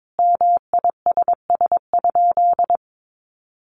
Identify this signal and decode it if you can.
Morse code, heard as a keyed tone, about 22 wpm, 700 Hz. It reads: MISS?
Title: Sentences from Top 500 Words 22wpm